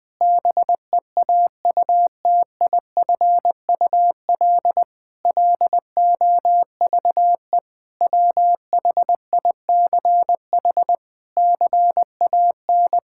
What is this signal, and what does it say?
Morse code, heard as a keyed tone, about 20 wpm, 700 Hz. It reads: BEAUTIFUL LOVE WHICH CAN
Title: Sentences from Top 1600 Words 20wpm